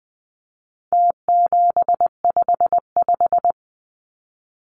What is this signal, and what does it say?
Morse code, heard as a keyed tone, about 20 wpm, 700 Hz. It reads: T755
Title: Numbers - Contest Style - 4-Digit 20wpm